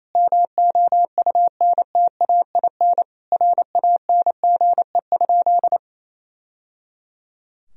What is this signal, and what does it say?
Morse code, heard as a keyed tone, about 28 wpm, 700 Hz. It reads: MOUNTAIN RANGE?
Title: Sentences from Top 1000 Words - Review Part 4 - 28wpm